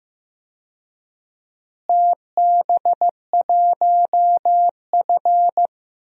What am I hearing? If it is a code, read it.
Morse code, heard as a keyed tone, about 15 wpm, 700 Hz. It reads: TB1F
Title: DXCC-100 Call Signs - 4 Characters - Repeated 3x - 15wpm